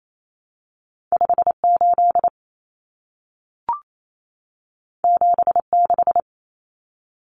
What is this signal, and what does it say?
Morse code, heard as a keyed tone, about 28 wpm, 700 Hz. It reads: 58 76
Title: Numbers - 2-Digit 28wpm